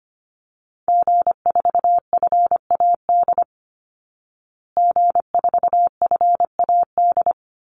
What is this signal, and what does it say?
Morse code, heard as a keyed tone, about 25 wpm, 700 Hz. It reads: G4FAD G4FAD
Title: Call Signs Repeated 2x - 25wpm